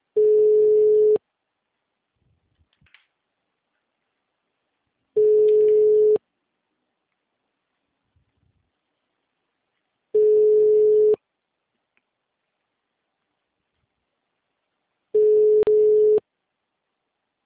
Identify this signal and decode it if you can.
Ringback tone (Europe, 425 Hz).